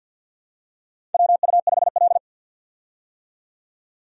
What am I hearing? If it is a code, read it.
Morse code, heard as a keyed tone, about 52 wpm, 700 Hz. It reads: WU5L
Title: USA Call Signs - 4 Characters - 50wpm